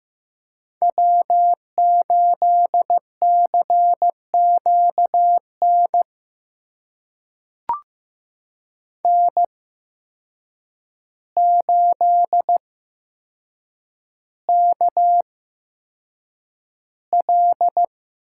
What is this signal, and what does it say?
Morse code, heard as a keyed tone, about 15 wpm, spaced as for 7 wpm, 700 Hz. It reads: W8CQN N8KL
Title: USA Call Signs - 5 Characters - 8x Character Spacing - 15wpm